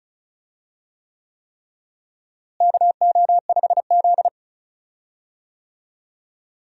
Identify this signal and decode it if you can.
Morse code, heard as a keyed tone, about 34 wpm, 700 Hz. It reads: KO5Z
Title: USA Call Signs - 4 Characters - Repeated 3x - 35wpm